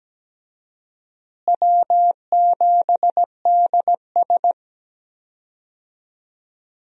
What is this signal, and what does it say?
Morse code, heard as a keyed tone, about 17 wpm, 700 Hz. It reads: W7DS
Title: USA Call Signs - 4 Characters - Repeated 3x - 17wpm